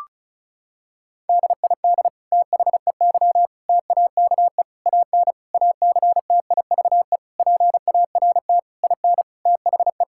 Morse code, heard as a keyed tone, about 36 wpm, 700 Hz. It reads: DID THEY TAKE AN ACTIVE PART IN THE
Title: Sentences from Top 1000 Words - Review Part 6 - 35wpm